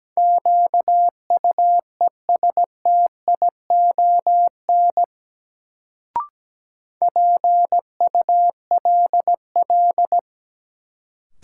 Morse code, heard as a keyed tone, about 17 wpm, 700 Hz. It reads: QUESTION PULL
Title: Sentences from Top 400 Words - New Word Drill - 17wpm